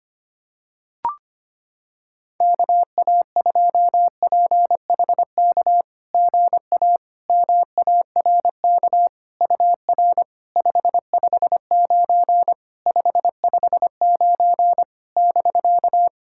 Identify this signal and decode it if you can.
Morse code, heard as a keyed tone, about 25 wpm, 690 Hz. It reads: KA2PHK GA MARK UR 559 559 <BK>
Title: SOTA QSO Practice 25wpm